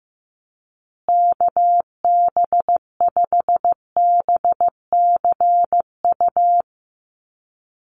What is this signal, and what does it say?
Morse code, heard as a keyed tone, about 15 wpm, 700 Hz. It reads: KB5BCU
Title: USA Call Signs Repeated 2x 15wpm